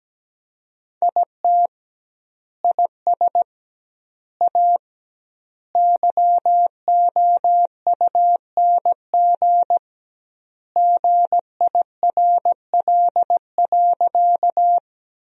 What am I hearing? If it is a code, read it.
Morse code, heard as a keyed tone, about 17 wpm, 700 Hz. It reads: IT IS A YOUNG GIRL.
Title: 2x Word-Spacing - Sentences from Top 400 Words - 17wpm